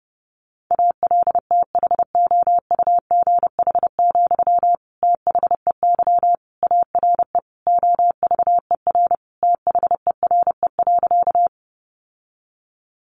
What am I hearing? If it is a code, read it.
Morse code, heard as a keyed tone, about 30 wpm, 700 Hz. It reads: ALTHOUGH, THEY ARE OVER THERE.
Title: Sentences from Top 600 Words 30wpm